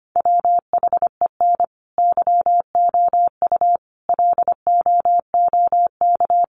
Morse code, heard as a keyed tone, about 25 wpm, 700 Hz. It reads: WHEN YOU LOOK
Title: Sentences from Top 800 Words 25wpm